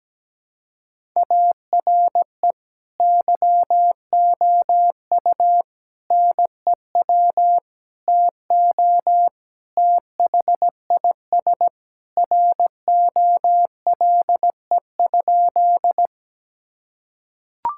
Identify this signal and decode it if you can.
Morse code, heard as a keyed tone, about 17 wpm, 700 Hz. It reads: ARE YOU NEW TO THIS ROLE?
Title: Sentences from Top 1000 Words - Review Part 6 - 17wpm